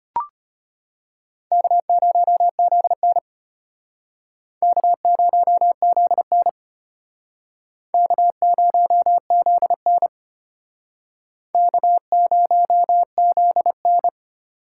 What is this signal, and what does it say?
Morse code, heard as a keyed tone, about 31 wpm, 700 Hz. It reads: K0ZN K0ZN K0ZN K0ZN
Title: Speed Racing - USA 1x2 Call Signs 25wpm